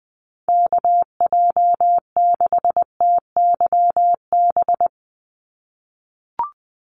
Morse code, heard as a keyed tone, about 20 wpm, 700 Hz. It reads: KJ6TYB
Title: USA Call Signs 20wpm